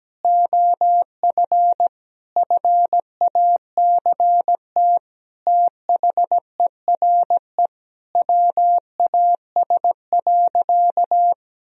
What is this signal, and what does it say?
Morse code, heard as a keyed tone, about 17 wpm, 700 Hz. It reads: OF FACT THERE WAS.
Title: Sentences from Top 1000 Words - Review Part 3 - 17wpm